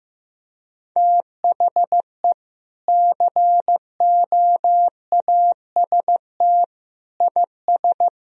Morse code, heard as a keyed tone, about 15 wpm, 700 Hz. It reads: THE COAST IS
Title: Sentences from Top 1000 Words - Review Part 4 - 15wpm